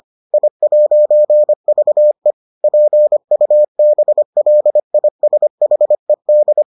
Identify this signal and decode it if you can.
Morse code, heard as a keyed tone, about 25 wpm, 600 Hz. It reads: I'VE PUBLISHED